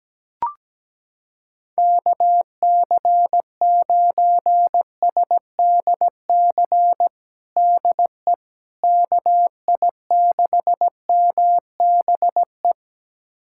Morse code, heard as a keyed tone, about 17 wpm, 700 Hz. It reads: KC9SDC DE KI6MBE
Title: USA Callsign DE Callsign - 6 Characters - 17wpm